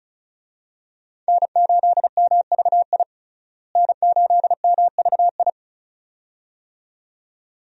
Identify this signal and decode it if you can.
Morse code, heard as a keyed tone, about 36 wpm, 700 Hz. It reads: N8MVI N8MVI
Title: USA Call Signs - 5 Characters - Repeated 2x - 35wpm